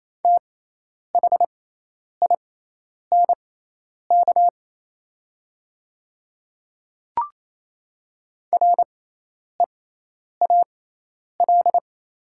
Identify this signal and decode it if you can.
Morse code, heard as a keyed tone, about 28 wpm, spaced as for 10 wpm, 700 Hz. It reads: THINK REAL
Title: 6x Character Spacing - Top 300 Words - 28wpm